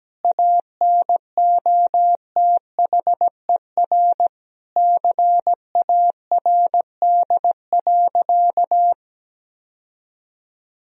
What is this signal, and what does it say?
Morse code, heard as a keyed tone, about 17 wpm, 700 Hz. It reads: ANOTHER CARD.
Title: Sentences from Top 1000 Words - Review Part 4 - 17wpm